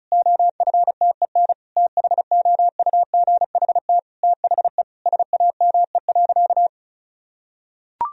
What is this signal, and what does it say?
Morse code, heard as a keyed tone, about 34 wpm, 700 Hz. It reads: OFTEN THOUGHT THE SAME.
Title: Sentences from Top 1000 Words - Review Part 1 - 35wpm